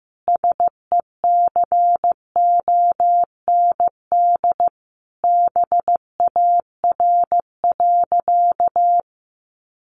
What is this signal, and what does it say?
Morse code, heard as a keyed tone, about 15 wpm, 700 Hz. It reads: SECOND BAR.
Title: Easy Phrases 15wpm